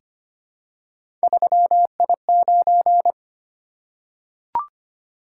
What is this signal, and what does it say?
Morse code, heard as a keyed tone, about 25 wpm, 700 Hz. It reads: 3I9
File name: Three Letter-Number Combinations - 25wpm